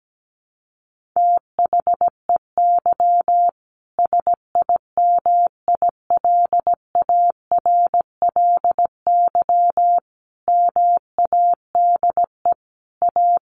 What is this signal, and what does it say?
Morse code, heard as a keyed tone, about 17 wpm, 700 Hz. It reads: THEY SIMILARLY MADE A
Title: Sentences from Top 1400 Words 17wpm